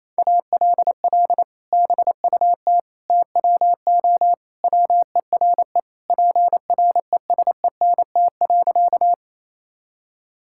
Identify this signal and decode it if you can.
Morse code, heard as a keyed tone, about 28 wpm, 700 Hz. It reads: ALL BUT TWO WERE PRESENT.
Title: Sentences from Top 1000 Words - Review Part 5 - 28wpm